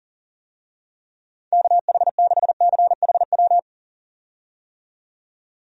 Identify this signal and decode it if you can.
Morse code, heard as a keyed tone, about 37 wpm, 700 Hz. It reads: KH6CHW
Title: USA Call Signs - 6 Characters - 40wpm